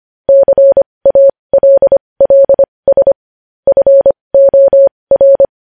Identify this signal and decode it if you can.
Morse code, heard as a keyed tone, about 25 wpm, spaced as for 20 wpm, 550 Hz. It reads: CALLS FOR
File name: podcast / News headlines-1x-25x20